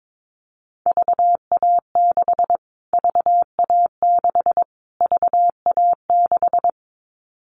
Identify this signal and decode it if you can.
Morse code, heard as a keyed tone, about 22 wpm, 700 Hz. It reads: VA6 VA6 VA6
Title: Callsign Prefixes for Top 25 DXCC - Repeated 3x - 22wpm